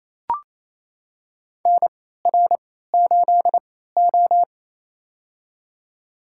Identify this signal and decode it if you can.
Morse code, heard as a keyed tone, about 28 wpm, spaced as for 16 wpm, 700 Hz. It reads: NR8O
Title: USA Call Signs - 4 Characters - 3x Character Spacing - 28wpm